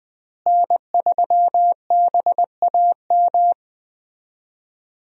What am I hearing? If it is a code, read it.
Morse code, heard as a keyed tone, about 20 wpm, 700 Hz. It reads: N3BAM
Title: USA Call Signs - 5 Characters - Repeated 2x - 20wpm